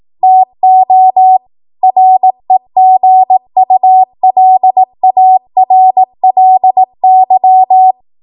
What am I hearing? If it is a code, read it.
Morse code, heard as a keyed tone, about 18 wpm, 750 Hz. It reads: TO REGULARLY